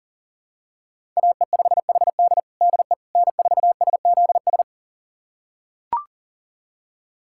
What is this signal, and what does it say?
Morse code, heard as a keyed tone, about 37 wpm, 700 Hz. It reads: AE5HD DE N4SZS
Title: USA Callsign DE Callsign - 5 Characters - 40wpm